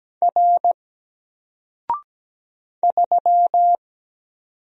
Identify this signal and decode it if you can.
Morse code, heard as a keyed tone, about 17 wpm, 700 Hz. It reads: R 3
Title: Single Letter-Number Alt 17wpm